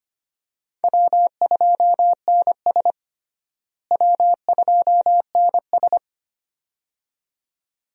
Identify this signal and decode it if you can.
Morse code, heard as a keyed tone, about 25 wpm, 700 Hz. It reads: W2NS W2NS
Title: USA Call Signs - 4 Characters - Repeated 2x - 25wpm